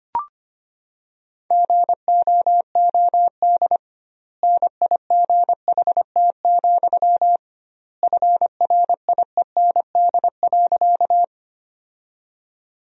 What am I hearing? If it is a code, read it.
Morse code, heard as a keyed tone, about 25 wpm, 700 Hz. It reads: GOOD NIGHT, FRIEND.
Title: 2x Word-Spacing - Sentences from Top 300 Words - 25wpm